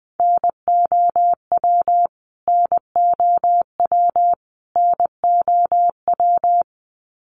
Morse code, heard as a keyed tone, about 20 wpm, 700 Hz. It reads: NOW NOW NOW
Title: Top 300 Words Repeated 20wpm